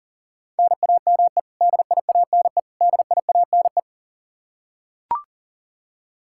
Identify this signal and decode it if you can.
Morse code, heard as a keyed tone, about 37 wpm, 700 Hz. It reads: NAME DIANE DIANE
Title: Practice QSO - Exchange 1 - Beginning - 40wpm